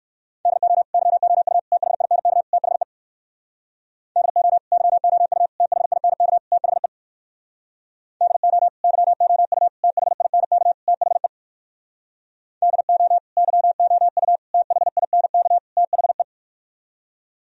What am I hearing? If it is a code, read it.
Morse code, heard as a keyed tone, about 64 wpm, 700 Hz. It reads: DO YOU THINK THE DO YOU THINK THE DO YOU THINK THE DO YOU THINK THE